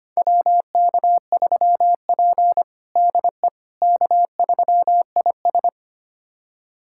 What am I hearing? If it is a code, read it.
Morse code, heard as a keyed tone, about 25 wpm, 700 Hz. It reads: WK3P DE K3IS
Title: USA Callsign DE Callsign - 4 Characters - 25wpm